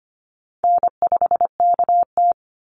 Morse code, heard as a keyed tone, about 25 wpm, 700 Hz. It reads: N5KT
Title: Easy Call Signs - 25wpm